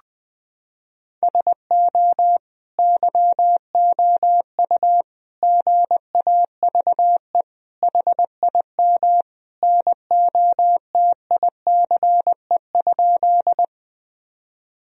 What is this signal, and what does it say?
Morse code, heard as a keyed tone, about 20 wpm, 700 Hz. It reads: SO YOU GAVE HIM NOTICE?
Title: Sentences from Top 1000 Words - Review Part 2 - 20wpm